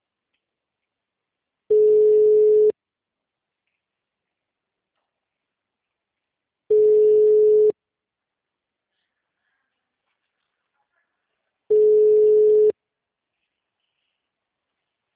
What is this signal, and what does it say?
Ringback tone (Europe, 425 Hz).